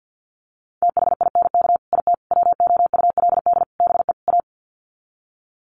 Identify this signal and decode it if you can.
Morse code, heard as a keyed tone, about 50 wpm, 700 Hz. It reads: THINK IT WOULD BE A
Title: Sets of 5 Words Encore2 50wpm